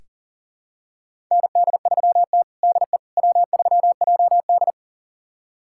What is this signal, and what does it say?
Morse code, heard as a keyed tone, about 40 wpm, 700 Hz. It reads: ND3T DE W3JD
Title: USA Callsign DE Callsign - 4 Characters - 40wpm